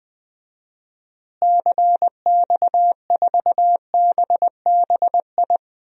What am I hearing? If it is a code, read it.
Morse code, heard as a keyed tone, about 21 wpm, 700 Hz. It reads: CX4BBI
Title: DXCC-100 Call Signs - 6 Characters - Repeated 2x - 20wpm